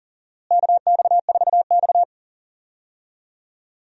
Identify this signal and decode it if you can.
Morse code, heard as a keyed tone, about 39 wpm, 700 Hz. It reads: KX4X
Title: USA Call Signs - 4 Characters - Repeated 3x - 40wpm